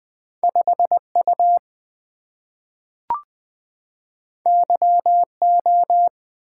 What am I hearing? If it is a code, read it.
Morse code, heard as a keyed tone, about 20 wpm, 700 Hz. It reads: 5 U Y O
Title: Two Letter-Number Combinations - 20wpm